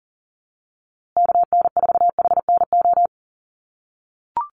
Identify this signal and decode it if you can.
Morse code, heard as a keyed tone, about 40 wpm, 700 Hz. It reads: KN4HNO
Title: USA Call Signs Repeated 3x 40wpm